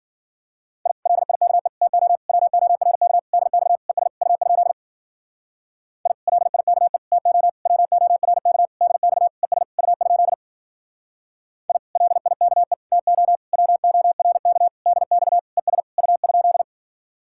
Morse code, heard as a keyed tone, about 50 wpm, 700 Hz. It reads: I LIKE TO WORK DX ES U? I LIKE TO WORK DX ES U? I LIKE TO WORK DX ES U?